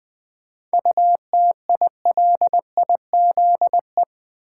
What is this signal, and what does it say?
Morse code, heard as a keyed tone, about 20 wpm, 700 Hz. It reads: UTILIZE
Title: 7 Letter Words - First 3 Letters Given - 20wpm